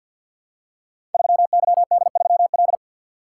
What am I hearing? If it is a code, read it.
Morse code, heard as a keyed tone, about 46 wpm, 700 Hz. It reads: 3YD3L
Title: Code Groups - 5 Letter:Numbers - 50wpm